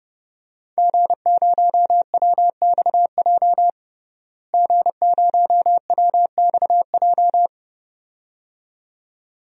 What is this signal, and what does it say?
Morse code, heard as a keyed tone, about 30 wpm, 700 Hz. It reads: G0WXJ G0WXJ
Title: DXCC-100 Call Signs - 5 Characters - Repeated 2x - 30wpm